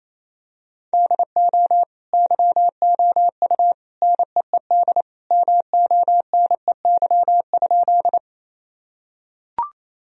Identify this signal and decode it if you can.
Morse code, heard as a keyed tone, about 28 wpm, 700 Hz. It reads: DO YOU NEED MONEY?
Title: Sentences from Top 1000 Words - Review Part 1 - 28wpm